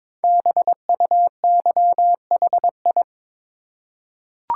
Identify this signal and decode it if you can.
Morse code, heard as a keyed tone, about 22 wpm, 700 Hz. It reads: BUYHI
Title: Code Groups - 5 Letters - 22wpm